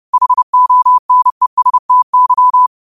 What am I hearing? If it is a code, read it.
Morse code, heard as a keyed tone, about 30 wpm, 1000 Hz. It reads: HONESTY
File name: qotdfiles / 30wpm